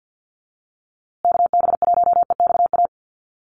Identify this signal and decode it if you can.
Morse code, heard as a keyed tone, about 50 wpm, 700 Hz. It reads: KB1EXA
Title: USA Call Signs Repeated 3x 50wpm